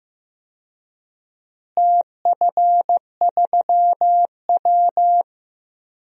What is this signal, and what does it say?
Morse code, heard as a keyed tone, about 15 wpm, 700 Hz. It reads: TF3W
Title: DXCC-100 Call Signs - 4 Characters - Repeated 3x - 15wpm